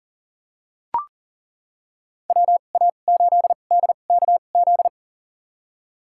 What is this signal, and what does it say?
Morse code, heard as a keyed tone, about 40 wpm, spaced as for 29 wpm, 700 Hz. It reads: WA8DKZ
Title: USA Call Signs - 6 Characters - 2x Character Spacing - 40wpm